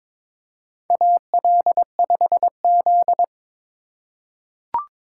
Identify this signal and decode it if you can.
Morse code, heard as a keyed tone, about 22 wpm, 700 Hz. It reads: AL5Z
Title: USA Call Signs - 4 Characters - 4x Character Spacing - 22wpm